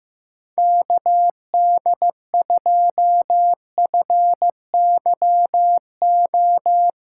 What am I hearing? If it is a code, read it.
Morse code, heard as a keyed tone, about 15 wpm, 700 Hz. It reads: KD2FYO
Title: USA Call Signs - 4x Character Spacing - 15wpm